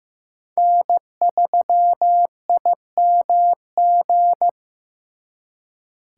Morse code, heard as a keyed tone, about 18 wpm, 700 Hz. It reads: N3IMG
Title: Speed Racing - USA 1x3 Call Signs 15wpm